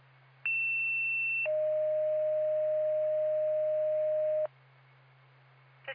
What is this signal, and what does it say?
Two-tone page, heard as a one-second tone, then a three-second tone, about 2700 then 620 Hz. It is heard from 0.4 to 4.5 s.